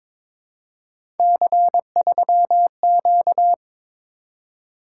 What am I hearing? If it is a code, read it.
Morse code, heard as a keyed tone, about 22 wpm, 700 Hz. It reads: C3Q
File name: practice / 22wpm / Three Letter-Number Combinations - 22wpm